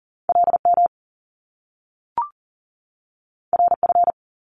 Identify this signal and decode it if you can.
Morse code, heard as a keyed tone, about 41 wpm, 700 Hz. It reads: LM RF
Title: Top Two Letter Combinations in Words 40wpm